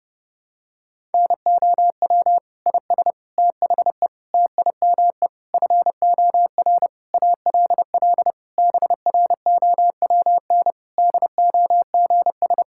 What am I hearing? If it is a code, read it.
Morse code, heard as a keyed tone, about 30 wpm, 700 Hz. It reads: NOW IS THE TIME FOR ALL BROWN DOGS
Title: Pangram Challenge v2 - 30wpm